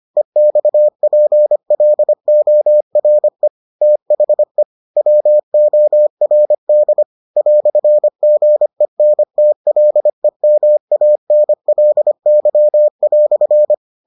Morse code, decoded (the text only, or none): EXPLORE THE WORD "GENTLEMANLY"